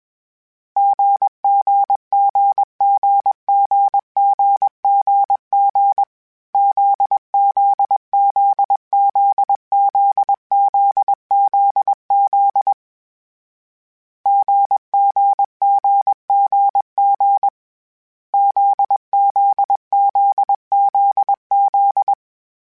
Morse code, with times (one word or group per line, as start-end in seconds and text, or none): 0.8-6.0 GGGGGGGG
6.5-12.7 ZZZZZZZZ
14.3-17.5 GGGGG
18.3-22.1 ZZZZZ